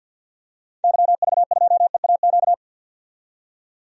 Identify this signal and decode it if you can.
Morse code, heard as a keyed tone, about 52 wpm, 700 Hz. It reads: YV2EAQ
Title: DXCC-100 Call Signs - 6 Characters - 50wpm